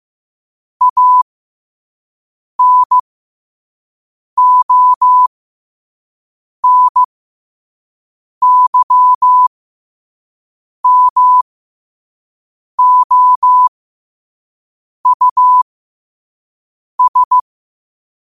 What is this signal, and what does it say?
Morse code, heard as a keyed tone, about 15 wpm, spaced as for 5 wpm, 1000 Hz. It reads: ANONYMOUS